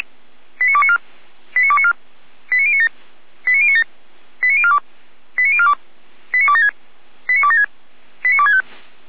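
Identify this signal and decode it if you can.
Five-tone selective call (ZVEI), heard as quick runs of five tones, tones of about 70 ms. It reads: 89294 89294 89097 89097 89042 89042 89276 89276 89265
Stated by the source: ZVEI five-tone selective call received on an SDR, from the Feuerwehr Orlamünde fire brigade.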